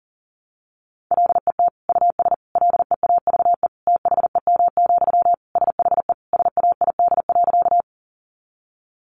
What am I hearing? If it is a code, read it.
Morse code, heard as a keyed tone, about 40 wpm, 700 Hz. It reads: LET US LEAVE THEM, SHE SAID.
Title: Sentences from Top 400 Words 40wpm